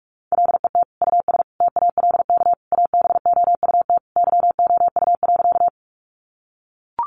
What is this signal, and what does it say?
Morse code, heard as a keyed tone, about 44 wpm, 700 Hz. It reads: LET US TALK ABOUT YOU.
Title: Sentences from Top 400 Words 45wpm